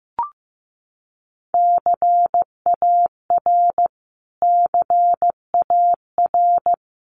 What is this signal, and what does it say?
Morse code, heard as a keyed tone, about 15 wpm, 700 Hz. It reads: CAR CAR
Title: Top 300 Words Repeated 15wpm